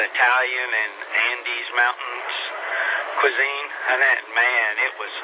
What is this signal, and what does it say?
Emergency Alert System (SAME) tones, heard as harsh buzzing bursts, with speech. It is heard from 0.9 to 4.9 s.